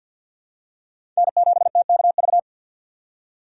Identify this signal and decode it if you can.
Morse code, heard as a keyed tone, about 52 wpm, 700 Hz. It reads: N7TKV
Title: USA Call Signs - 3x Character Spacing - 50wpm